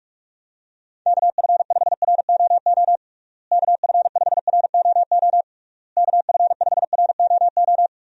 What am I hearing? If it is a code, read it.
Morse code, heard as a keyed tone, about 46 wpm, 700 Hz. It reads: KF5ROO KF5ROO KF5ROO
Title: USA Call Signs - 6 Characters - Repeated 3x - 45wpm